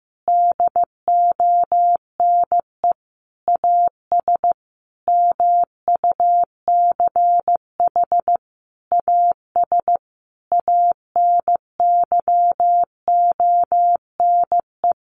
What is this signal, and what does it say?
Morse code, heard as a keyed tone, about 15 wpm, 700 Hz. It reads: DONE AS MUCH AS ANYONE
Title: Sentences from Top 1200 Words 15wpm